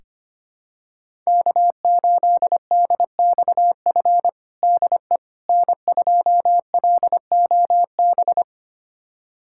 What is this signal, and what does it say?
Morse code, heard as a keyed tone, about 25 wpm, 700 Hz. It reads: K8DXF DE N2LOB
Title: USA Callsign DE Callsign - 5 Characters - 25wpm